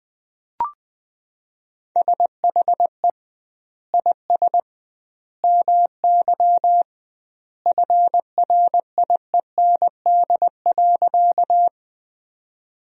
Morse code, heard as a keyed tone, about 20 wpm, 700 Hz. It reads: SHE IS MY FRIEND.
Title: 2x Word-Spacing - Sentences from Top 300 Words - 20wpm